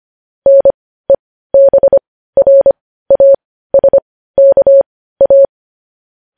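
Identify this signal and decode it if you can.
Morse code, heard as a keyed tone, about 25 wpm, spaced as for 15 wpm, 550 Hz. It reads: NEBRASKA
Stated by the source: Morse code news headlines sent at 15.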